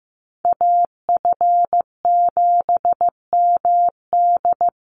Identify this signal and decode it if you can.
Morse code, heard as a keyed tone, about 15 wpm, 700 Hz. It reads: AF7MD
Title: NA SOTA Chaser Callsigns 15wpm